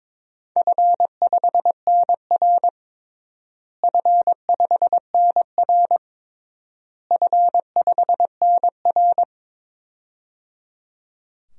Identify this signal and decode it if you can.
Morse code, heard as a keyed tone, about 22 wpm, 700 Hz. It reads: F5NR F5NR F5NR
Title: DXCC-100 Call Signs - 4 Characters - Repeated 3x - 22wpm